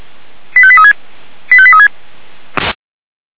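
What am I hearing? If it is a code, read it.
Five-tone selective call (ZVEI), heard as quick runs of five tones, tones of about 70 ms. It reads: 85836 85836
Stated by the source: ZVEI five-tone selective call received by SDR (464).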